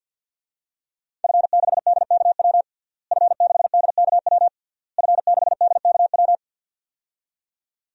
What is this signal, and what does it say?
Morse code, heard as a keyed tone, about 51 wpm, 700 Hz. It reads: F6DKW F6DKW F6DKW
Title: DXCC-100 Call Signs - 5 Characters - Repeated 3x - 50wpm